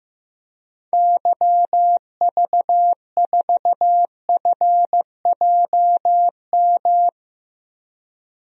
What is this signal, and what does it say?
Morse code, heard as a keyed tone, about 15 wpm, 700 Hz. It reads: YV4FJM
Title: DXCC-100 Call Signs - 6 Characters - Repeated 2x - 15wpm